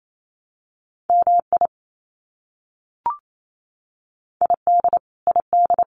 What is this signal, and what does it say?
Morse code, heard as a keyed tone, about 28 wpm, 700 Hz. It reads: MI ID ID
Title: US States Repeated 3x 28wpm